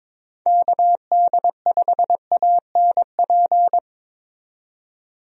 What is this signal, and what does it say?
Morse code, heard as a keyed tone, about 21 wpm, 700 Hz. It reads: KD5ANP
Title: USA Call Signs - 6 Characters - 22wpm